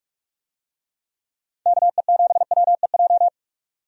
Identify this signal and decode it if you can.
Morse code, heard as a keyed tone, about 44 wpm, 700 Hz. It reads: KE7WEJ
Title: USA Call Signs - 6 Characters - 45wpm